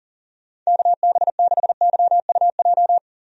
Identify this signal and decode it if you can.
Morse code, heard as a keyed tone, about 40 wpm, 700 Hz. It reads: KB6YUJ
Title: USA Call Signs - 6 Characters - Repeated 2x - 40wpm